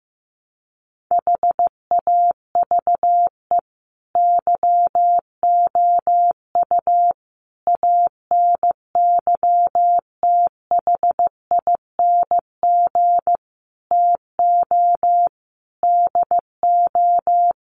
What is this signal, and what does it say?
Morse code, heard as a keyed tone, about 15 wpm, 700 Hz. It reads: HAVE YOU ANYTHING TO DO
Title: Sentences from Top 1000 Words 15wpm